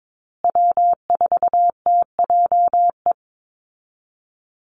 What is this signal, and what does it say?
Morse code, heard as a keyed tone, about 22 wpm, 700 Hz. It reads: W4TJE
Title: Call Signs Repeated 2x - 22wpm